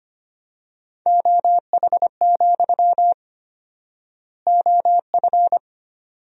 Morse code, heard as a keyed tone, about 25 wpm, 700 Hz. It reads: OH, OF
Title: 4x Word-Spacing - Sentences from Top 500 Words - 25wpm